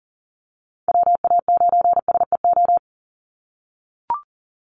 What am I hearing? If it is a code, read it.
Morse code, heard as a keyed tone, about 40 wpm, 700 Hz. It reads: WA9SEO
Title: Call Signs - 40wpm